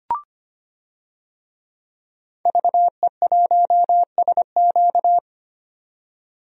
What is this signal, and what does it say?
Morse code, heard as a keyed tone, about 25 wpm, 700 Hz. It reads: VE1SQ
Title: DXCC-100 Call Signs - 5 Characters - 25wpm